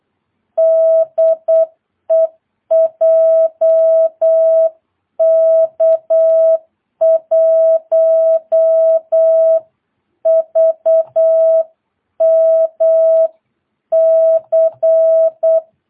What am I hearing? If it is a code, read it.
Morse code, heard as a keyed tone, about 8 wpm, 650 Hz. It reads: DE JK1VMC
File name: jh8cbh / 2022cw